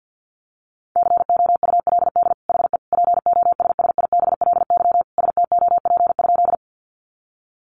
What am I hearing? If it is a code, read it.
Morse code, heard as a keyed tone, about 50 wpm, 700 Hz. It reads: COULD HE POSSIBLY STOP?
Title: Sentences from Top 1300 Words 50wpm